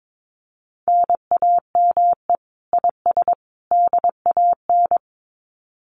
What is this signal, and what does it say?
Morse code, heard as a keyed tone, about 22 wpm, 700 Hz. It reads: NAME IS DAN
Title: Common QSO Phrases 22wpm